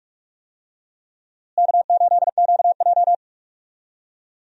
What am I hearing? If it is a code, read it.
Morse code, heard as a keyed tone, about 42 wpm, 700 Hz. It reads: K8QJ
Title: USA Call Signs - 4 Characters - Repeated 3x - 45wpm